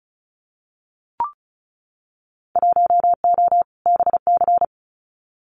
Morse code, heard as a keyed tone, about 36 wpm, 700 Hz. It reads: 1O BC
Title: Winter Field Day Exchanges Encore 35wpm